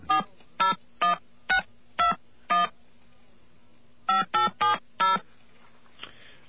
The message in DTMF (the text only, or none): 401621 5870